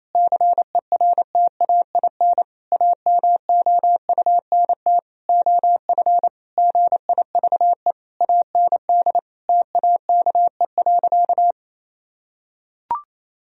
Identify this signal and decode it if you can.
Morse code, heard as a keyed tone, about 28 wpm, 700 Hz. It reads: CERTAIN AMOUNT OF GIVE AND TAKE.
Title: Sentences from Top 1000 Words - Review Part 4 - 28wpm